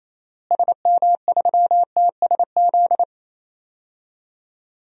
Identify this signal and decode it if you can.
Morse code, heard as a keyed tone, about 27 wpm, 700 Hz. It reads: SM3TSZ
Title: DXCC-100 Call Signs - 6 Characters - Repeated 2x - 28wpm